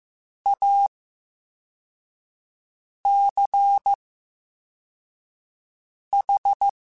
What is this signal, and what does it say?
Morse code, heard as a keyed tone, about 15 wpm, 800 Hz. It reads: A C H
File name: Special INGs-mp3s / Special INGs-mp3s / VerySlowToFaster Ing @13